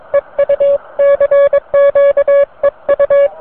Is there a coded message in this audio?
Yes: Morse code from 0.1 to 3.3 s, heard as a keyed tone, about 22 wpm, 560 Hz. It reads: EU CQ EU